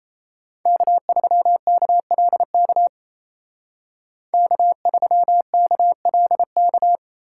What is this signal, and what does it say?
Morse code, heard as a keyed tone, about 36 wpm, 700 Hz. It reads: K3KLK K3KLK